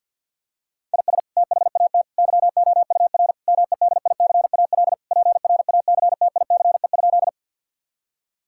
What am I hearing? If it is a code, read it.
Morse code, heard as a keyed tone, about 52 wpm, 700 Hz. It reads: IS THAT YOUR MEDICAL PRACTICE?
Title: Sentences from Top 1000 Words - Review Part 5 - 50wpm